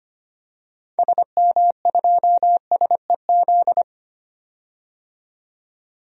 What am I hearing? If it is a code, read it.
Morse code, heard as a keyed tone, about 25 wpm, 700 Hz. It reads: SM2SEZ
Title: DXCC-100 Call Signs - 6 Characters - 25wpm